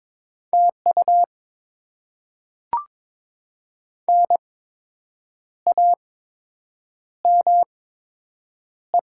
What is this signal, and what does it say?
Morse code, heard as a keyed tone, about 22 wpm, spaced as for 7 wpm, 700 Hz. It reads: TU NAME
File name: Top 20 QSO Elements - 8x Character Spacing - 22wpm